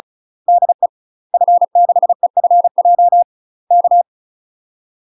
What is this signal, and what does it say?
Morse code, heard as a keyed tone, about 36 wpm, 700 Hz. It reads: DE F6EFJ K
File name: UK MASTER QSOs TRAINING / 35WPM / UK MASTER QSOs 0012